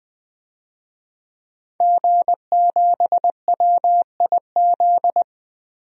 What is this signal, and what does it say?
Morse code, heard as a keyed tone, about 21 wpm, 700 Hz. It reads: G7WIZ
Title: DXCC-100 Call Signs - 5 Characters - Repeated 3x - 20wpm